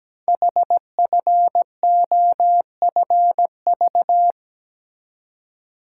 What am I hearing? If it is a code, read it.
Morse code, heard as a keyed tone, about 17 wpm, 700 Hz. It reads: HFOFV
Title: Code Groups - 5 Letters - 17wpm